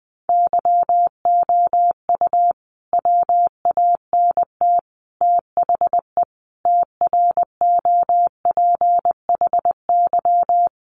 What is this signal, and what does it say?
Morse code, heard as a keyed tone, about 20 wpm, 700 Hz. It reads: YOU WANT THE TROPHY
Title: A Sound Of Thunder 20wpm